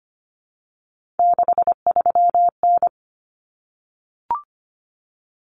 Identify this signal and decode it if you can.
Morse code, heard as a keyed tone, about 25 wpm, 700 Hz. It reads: 63N
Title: Common Cut Numbers - 3-Digit 25wpm